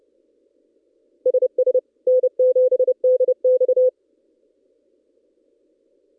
Morse code, heard as a keyed tone, about 30 wpm, 510 Hz. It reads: SS N7DX
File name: contests / ss-cw-k5tr-op-n5rz -1700